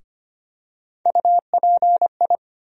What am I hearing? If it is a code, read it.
Morse code, heard as a keyed tone, about 25 wpm, 700 Hz. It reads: UPI
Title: USA Call Sign Postfixes - 3 Letters 25wpm